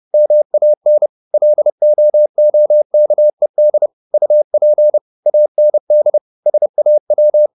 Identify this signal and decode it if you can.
Morse code, heard as a keyed tone, about 30 wpm, 600 Hz. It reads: MAN LOOKED UP AND SAW
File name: the old man and the sea / the old man and the sea 1 30wpm